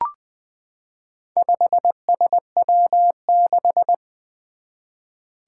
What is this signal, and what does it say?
Morse code, heard as a keyed tone, about 20 wpm, 700 Hz. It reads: 5SW6
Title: Four Challenging Letter-Number Combinations 20wpm